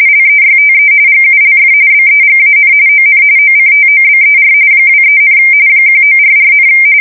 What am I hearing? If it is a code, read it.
RTTY, about 45.45 baud. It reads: LEYTE GULF BEGAN ON OCTOBER 23, 1944,
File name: fsk170 / battleshipnj